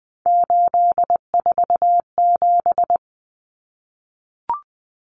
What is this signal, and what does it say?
Morse code, heard as a keyed tone, about 20 wpm, 700 Hz. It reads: 847
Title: Numbers - 3-Digit 20wpm